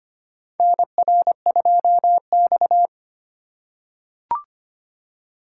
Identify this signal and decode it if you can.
Morse code, heard as a keyed tone, about 25 wpm, 700 Hz. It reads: NR2X
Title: USA Call Signs - 4 Characters - 4x Character Spacing - 25wpm